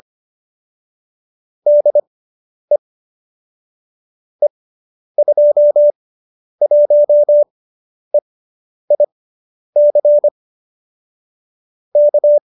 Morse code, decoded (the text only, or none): DE E21EIC K